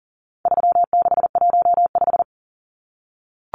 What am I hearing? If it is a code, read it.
Morse code, heard as a keyed tone, about 40 wpm, 700 Hz. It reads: 3615
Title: Numbers - 4-Digit 40wpm